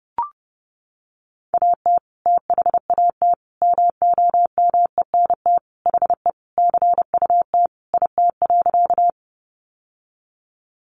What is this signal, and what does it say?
Morse code, heard as a keyed tone, about 30 wpm, 700 Hz. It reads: AT THAT MOMENT HE CUT IT.
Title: Sentences from Top 1000 Words 30wpm